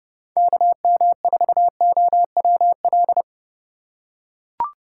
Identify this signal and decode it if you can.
Morse code, heard as a keyed tone, about 30 wpm, 700 Hz. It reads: KM4OWL